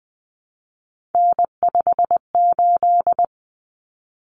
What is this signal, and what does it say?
Morse code, heard as a keyed tone, about 20 wpm, 700 Hz. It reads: N58
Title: Common Cut Numbers - 3-Digit 20wpm